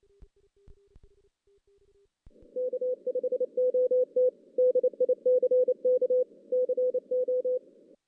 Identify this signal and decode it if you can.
Morse code, heard as a keyed tone, about 29 wpm, 500 Hz. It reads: K5OT DICK CO
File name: 2016 JAN CW NAQP / 2016 Jan naqp k5ot - 0-5